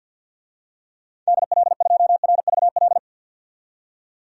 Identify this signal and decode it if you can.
Morse code, heard as a keyed tone, about 50 wpm, 700 Hz. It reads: DL2RVL
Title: DXCC-100 Call Signs - 6 Characters - 50wpm